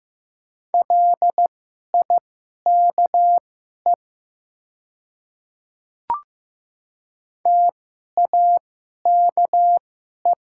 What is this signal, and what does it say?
Morse code, heard as a keyed tone, about 15 wpm, spaced as for 11 wpm, 700 Hz. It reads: LIKE TAKE
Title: 2x Character Spacing - Top 200 Words - 15wpm